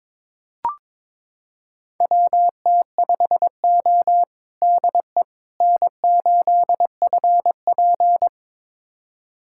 Morse code, heard as a keyed tone, about 22 wpm, 700 Hz. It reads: WT5O DE N8FP
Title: USA Callsign DE Callsign - 4 Characters - 22wpm